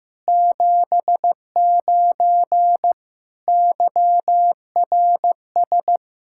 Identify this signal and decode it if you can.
Morse code, heard as a keyed tone, about 15 wpm, 700 Hz. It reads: 79 YRS
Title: Practice QSO - Exchange 3 - More Info - 15wpm